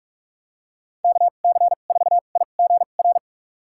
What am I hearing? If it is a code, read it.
Morse code, heard as a keyed tone, about 43 wpm, spaced as for 32 wpm, 700 Hz. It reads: KC4IGR
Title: USA Call Signs - 6 Characters - 2x Character Spacing - 45wpm